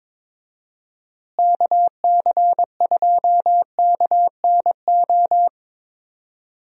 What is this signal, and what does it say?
Morse code, heard as a keyed tone, about 22 wpm, 700 Hz. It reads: KC2KNO
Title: USA Call Signs - 6 Characters - Repeated 2x - 22wpm